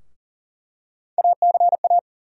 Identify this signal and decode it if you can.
Morse code, heard as a keyed tone, about 40 wpm, 700 Hz. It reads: ACA
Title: USA Call Sign Postfixes - 3 Letters 40wpm